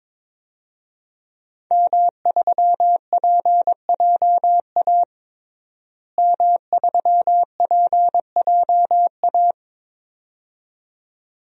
Morse code, heard as a keyed tone, about 22 wpm, 700 Hz. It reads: M3PJA M3PJA
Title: DXCC-100 Call Signs - 5 Characters - Repeated 2x - 22wpm